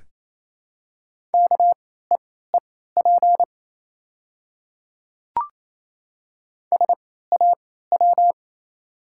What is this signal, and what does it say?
Morse code, heard as a keyed tone, about 28 wpm, spaced as for 16 wpm, 700 Hz. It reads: KEEP SAW
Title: 3x Character Spacing - Top 500 Words - 28wpm